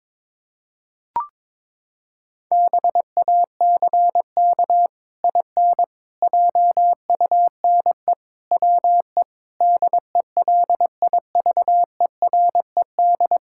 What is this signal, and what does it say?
Morse code, heard as a keyed tone, about 22 wpm, 700 Hz. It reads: BACK IN JUNE WE DELIVERED
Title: Pangram Challenge v2 - 22wpm